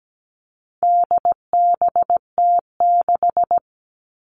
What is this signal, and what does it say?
Morse code, heard as a keyed tone, about 17 wpm, 700 Hz. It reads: DBT6
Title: All Cut Numbers - 4-Digit 17wpm